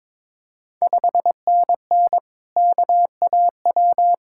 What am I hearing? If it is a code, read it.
Morse code, heard as a keyed tone, about 22 wpm, 700 Hz. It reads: 5NN KAW
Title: Ontario QSO Party Exchange 22wpm